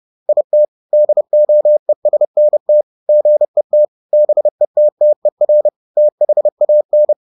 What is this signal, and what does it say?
Morse code, heard as a keyed tone, about 30 wpm, 600 Hz. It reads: IT DOESNT GET BETTER THAN